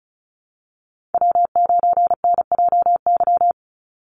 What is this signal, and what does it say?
Morse code, heard as a keyed tone, about 34 wpm, 700 Hz. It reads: W9NJY
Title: Call Signs - 35wpm